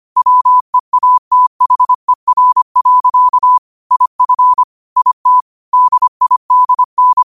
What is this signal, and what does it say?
Morse code, heard as a keyed tone, about 25 wpm, 1000 Hz. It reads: WEATHER. IF IT DIDN
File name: qotdfiles / 25wpm